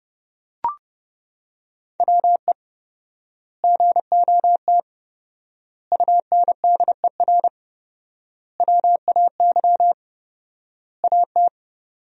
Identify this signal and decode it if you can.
Morse code, heard as a keyed tone, about 29 wpm, 700 Hz. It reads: WE GOT UNDER WAY AT
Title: 4x Word-Spacing - Sentences from Top 300 Words - 30wpm